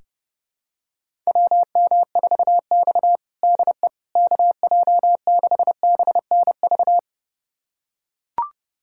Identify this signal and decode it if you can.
Morse code, heard as a keyed tone, about 32 wpm, 700 Hz. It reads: WM4X DE KJ6BNV
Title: USA Callsign DE Callsign 30wpm